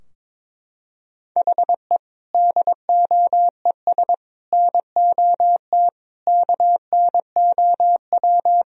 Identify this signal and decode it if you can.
Morse code, heard as a keyed tone, about 22 wpm, 700 Hz. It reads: HE DOES NOT KNOW
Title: Sentences from Top 1000 Words - Review Part 3 - 22wpm